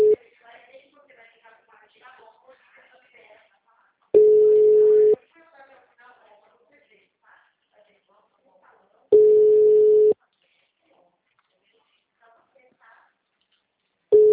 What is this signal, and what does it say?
Ringback tone (Europe, 425 Hz), with speech.